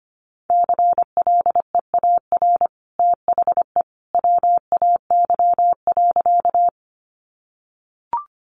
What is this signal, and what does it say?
Morse code, heard as a keyed tone, about 25 wpm, 700 Hz. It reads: CLEAR THE WAY.
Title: Sentences from Top 500 Words 25wpm